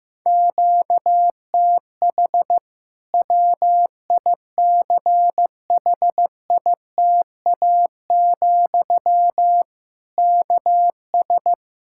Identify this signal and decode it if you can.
Morse code, heard as a keyed tone, about 15 wpm, 700 Hz. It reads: QTH WICHITA, KS
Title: Practice QSO - Exchange 1 - Beginning - 15wpm